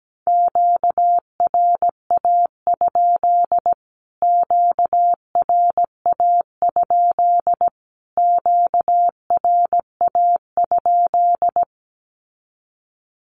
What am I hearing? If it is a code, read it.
Morse code, heard as a keyed tone, about 17 wpm, 700 Hz. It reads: QRA? QRA? QRA?
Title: Q-Codes Repeated 3x 17wpm